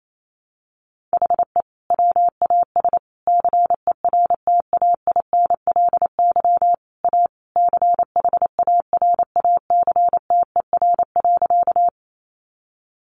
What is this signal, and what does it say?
Morse code, heard as a keyed tone, about 28 wpm, 700 Hz. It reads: HE WAS CERTAINLY A CHARACTER.
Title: Sentences from Top 1200 Words 28wpm